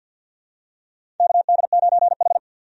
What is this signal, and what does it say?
Morse code, heard as a keyed tone, about 48 wpm, 700 Hz. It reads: KD9H
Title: USA Call Signs - 4 Characters - 50wpm